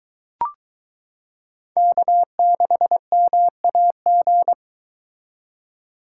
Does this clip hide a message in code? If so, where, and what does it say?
Yes: Morse code from 1.8 to 4.5 s, heard as a keyed tone, about 23 wpm, 700 Hz. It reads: K6MAG